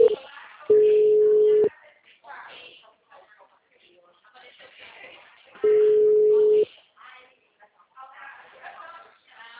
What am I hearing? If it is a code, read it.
Ringback tone (Europe, 425 Hz), with speech.